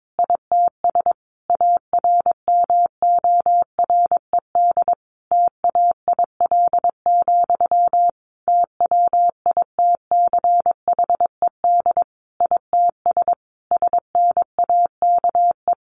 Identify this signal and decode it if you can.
Morse code, heard as a keyed tone, about 22 wpm, 700 Hz. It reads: ITS ARMORED TAIL, TWITCHED ITS SNAKE
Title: A Sound Of Thunder 22wpm